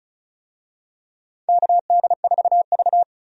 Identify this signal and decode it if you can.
Morse code, heard as a keyed tone, about 36 wpm, 700 Hz. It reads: KD4V
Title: USA Call Signs - 4 Characters - 35wpm